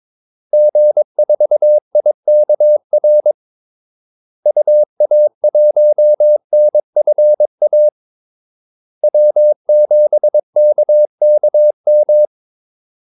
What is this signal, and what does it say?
Morse code, heard as a keyed tone, about 22 wpm, 600 Hz. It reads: G4IKR UA1NFA W7KKM
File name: Callsigns-22wpm